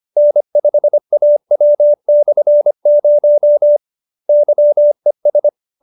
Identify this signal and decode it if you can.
Morse code, heard as a keyed tone, about 25 wpm, 570 Hz. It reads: N5AW/0 YES